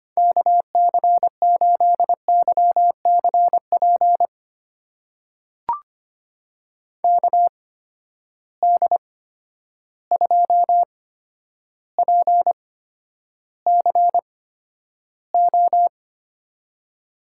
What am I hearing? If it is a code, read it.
Morse code, heard as a keyed tone, about 25 wpm, spaced as for 11 wpm, 700 Hz. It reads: KC8YCP KD2PCO
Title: USA Call Signs - 6 Characters - 8x Character Spacing - 25wpm